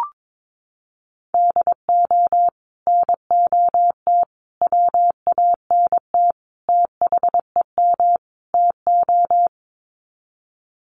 Morse code, heard as a keyed tone, about 22 wpm, 700 Hz. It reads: DO NOT WANT THEM TO
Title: Sets of 5 Words Encore 22wpm